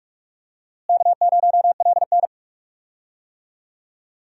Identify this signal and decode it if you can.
Morse code, heard as a keyed tone, about 46 wpm, 700 Hz. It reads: K0LN
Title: USA Call Signs - 4 Characters - Repeated 3x - 45wpm